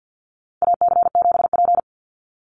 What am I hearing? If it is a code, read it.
Morse code, heard as a keyed tone, about 50 wpm, 700 Hz. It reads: AC7P
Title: NA SOTA Chaser Callsigns 50wpm